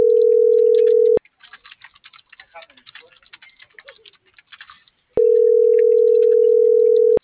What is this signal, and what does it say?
Ringback tone (North America), 440 + 480 Hz.